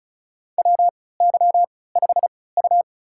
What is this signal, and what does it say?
Morse code, heard as a keyed tone, about 34 wpm, spaced as for 20 wpm, 700 Hz. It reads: WY5U
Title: USA Call Signs - 4 Characters - 3x Character Spacing - 35wpm